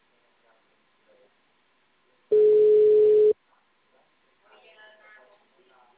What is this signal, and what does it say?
Ringback tone (Europe, 425 Hz), with speech.